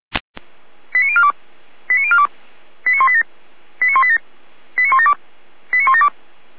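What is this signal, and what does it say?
Five-tone selective call (ZVEI), heard as quick runs of five tones, tones of about 70 ms. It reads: 89042 89042 89187 89187 89172 89172
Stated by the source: ZVEI five-tone call 89020 received by SDR.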